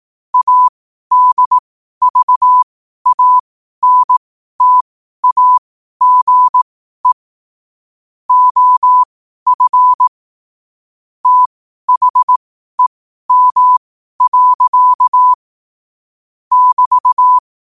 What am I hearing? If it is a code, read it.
Morse code, heard as a keyed tone, about 18 wpm, spaced as for 12 wpm, 1000 Hz. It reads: ADVANTAGE OF THEM. <BT>